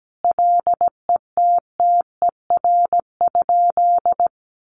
Morse code, heard as a keyed tone, about 17 wpm, 700 Hz. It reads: LETTER?
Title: Sentences from Top 300 Words 17wpm